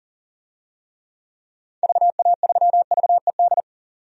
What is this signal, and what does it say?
Morse code, heard as a keyed tone, about 40 wpm, 700 Hz. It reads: VA3VED
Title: DXCC-100 Call Signs - 6 Characters - 40wpm